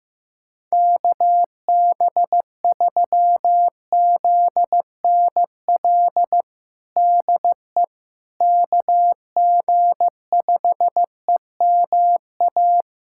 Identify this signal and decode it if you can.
Morse code, heard as a keyed tone, about 16 wpm, 700 Hz. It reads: KB3ZNL DE KG5EMA
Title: USA Callsign DE Callsign - 6 Characters - 15wpm